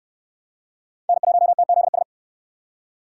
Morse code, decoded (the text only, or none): N1TZS